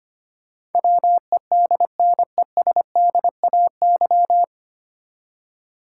Morse code, heard as a keyed tone, about 25 wpm, 700 Hz. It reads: WEDNESDAY
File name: Nine Letter Words - 25wpm